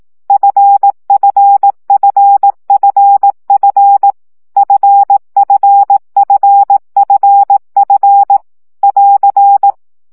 Morse code, heard as a keyed tone, about 18 wpm, 800 Hz. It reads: FFFFF FFFFF <AR>
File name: CW gyakorlas f 15 baud